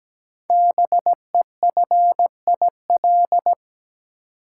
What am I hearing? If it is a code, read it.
Morse code, heard as a keyed tone, about 17 wpm, 700 Hz. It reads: BEFIL
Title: Code Groups - 5 Letters - 17wpm